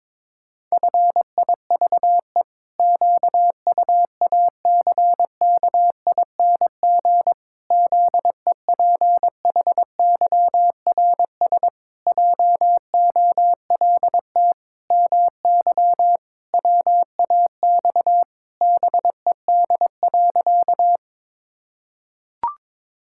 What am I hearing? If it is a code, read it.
Morse code, heard as a keyed tone, about 22 wpm, 700 Hz. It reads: FIVE QUACKING ZEPHYRS JOLT MY WAX BED.
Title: Pangram Challenge v2 - 22wpm